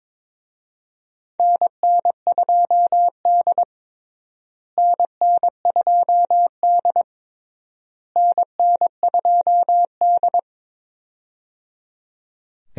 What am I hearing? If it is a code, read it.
Morse code, heard as a keyed tone, about 22 wpm, 700 Hz. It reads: NN2D NN2D NN2D
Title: USA Call Signs - 4 Characters - Repeated 3x - 22wpm